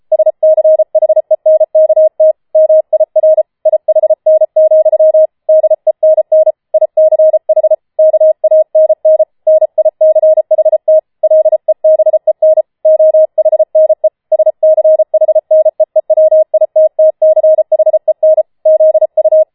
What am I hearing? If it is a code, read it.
Morse code, heard as a keyed tone, about 33 wpm, 630 Hz. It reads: SCHENKT MIR IHN, DENN ICH KANN NICHT LEBEN OHNE SCHNEEWITTCHEN ZU